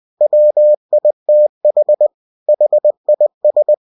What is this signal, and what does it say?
Morse code, heard as a keyed tone, about 20 wpm, 600 Hz. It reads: WITH HIS
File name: cw-books / winnie-the-pooh / winnie-20wpm-page-0096